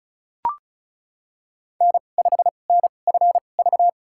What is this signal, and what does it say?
Morse code, heard as a keyed tone, about 36 wpm, spaced as for 25 wpm, 690 Hz. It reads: N5NFV
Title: USA Call Signs - 5 Characters - 2x Character Spacing - 35wpm